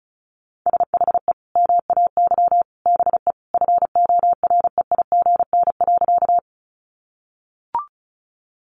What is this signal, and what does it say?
Morse code, heard as a keyed tone, about 34 wpm, 700 Hz. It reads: SHE MAY BE FOREIGN.
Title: Sentences from Top 1400 Words 35wpm